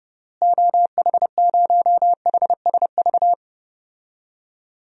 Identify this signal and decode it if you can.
Morse code, heard as a keyed tone, about 30 wpm, 700 Hz. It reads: OH0HSV
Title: DXCC-100 Call Signs - 6 Characters - Repeated 2x - 30wpm